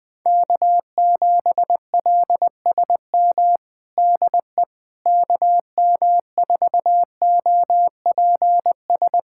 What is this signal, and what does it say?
Morse code, heard as a keyed tone, about 20 wpm, 700 Hz. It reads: K7LSM DE KM4OPS
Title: USA Callsign DE Callsign 20wpm